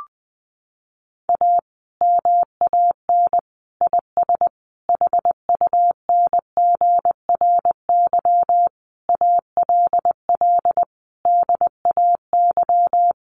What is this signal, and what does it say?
Morse code, heard as a keyed tone, about 20 wpm, 700 Hz. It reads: A MAN IS HUNGRY ALL DAY
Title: Sentences from Top 1400 Words 20wpm